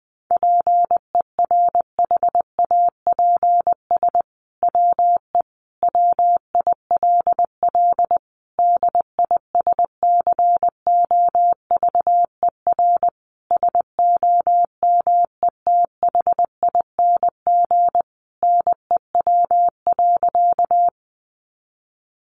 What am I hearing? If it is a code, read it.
Morse code, heard as a keyed tone, about 19 wpm, 700 Hz. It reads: PERHAPS WE WILL DISCOVER SOMETHING NEW.
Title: Sentences from Top 1100 Words 20wpm